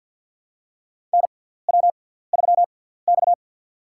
Morse code, heard as a keyed tone, about 50 wpm, spaced as for 17 wpm, 700 Hz. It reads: NW3X
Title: USA Call Signs - 4 Characters - 6x Character Spacing - 50wpm